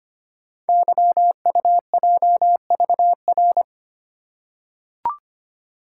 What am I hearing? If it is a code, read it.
Morse code, heard as a keyed tone, about 25 wpm, 690 Hz. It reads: YUJVR
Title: Code Groups - 5 Letters - 25wpm